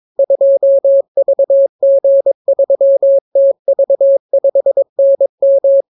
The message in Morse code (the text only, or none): 2VG3TV5NM